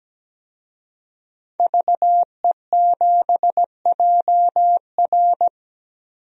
Morse code, heard as a keyed tone, about 17 wpm, 700 Hz. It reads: VE7JR
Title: DXCC-100 Call Signs - 5 Characters - Repeated 2x - 17wpm